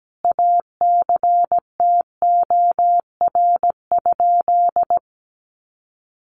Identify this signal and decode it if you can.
Morse code, heard as a keyed tone, about 17 wpm, 700 Hz. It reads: ACTOR?
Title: Sentences from Top 1500 Words 17wpm